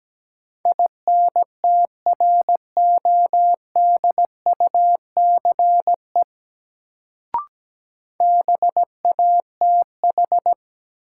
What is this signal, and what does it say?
Morse code, heard as a keyed tone, about 17 wpm, 700 Hz. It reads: INTRODUCE BATH
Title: Sentences from Top 1600 Words - New Word Drill - 17wpm